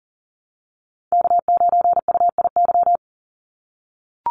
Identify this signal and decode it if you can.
Morse code, heard as a keyed tone, about 40 wpm, 700 Hz. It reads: K9UIY
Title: Call Signs - 40wpm